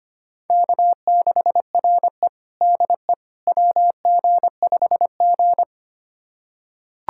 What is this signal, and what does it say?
Morse code, heard as a keyed tone, about 25 wpm, 700 Hz. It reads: K6RE DE WG5G
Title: USA Callsign DE Callsign - 4 Characters - 25wpm